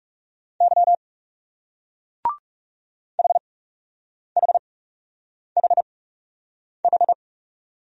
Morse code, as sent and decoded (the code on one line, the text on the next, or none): -.-- / .... .... .... ....
Y HHHH